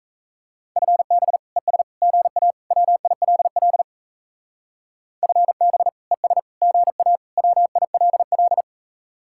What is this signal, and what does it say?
Morse code, heard as a keyed tone, about 36 wpm, 700 Hz. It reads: FB ES GA WILL FB ES GA WILL